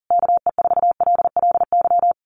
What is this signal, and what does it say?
Morse code, heard as a keyed tone, about 40 wpm, 700 Hz. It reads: KE4LLY
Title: USA Call Signs Repeated 3x 40wpm